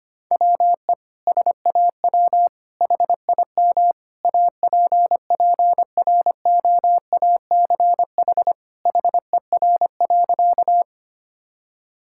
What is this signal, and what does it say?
Morse code, heard as a keyed tone, about 25 wpm, 700 Hz. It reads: WE SAW HIM APPROACH HER.
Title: Sentences from Top 1000 Words - Review Part 6 - 25wpm